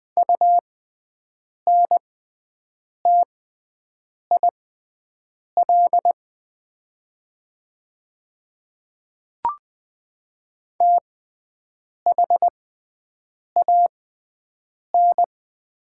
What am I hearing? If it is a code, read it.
Morse code, heard as a keyed tone, about 20 wpm, spaced as for 7 wpm, 700 Hz. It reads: UNTIL THAN
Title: 6x Character Spacing - Top 300 Words - 20wpm